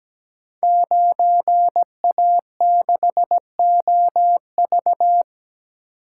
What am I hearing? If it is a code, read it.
Morse code, heard as a keyed tone, about 17 wpm, 700 Hz. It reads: 9A6OV
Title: Code Groups - 5 Letter:Numbers - 17wpm